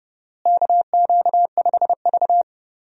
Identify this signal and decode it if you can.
Morse code, heard as a keyed tone, about 29 wpm, 700 Hz. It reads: KQ5V
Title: USA Call Signs - 4 Characters - Repeated 2x - 30wpm